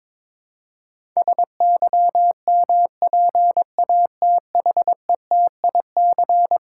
Morse code, heard as a keyed tone, about 22 wpm, 700 Hz. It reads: SYMPATHETIC
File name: practice / 22wpm / Eleven Letter Words - 22wpm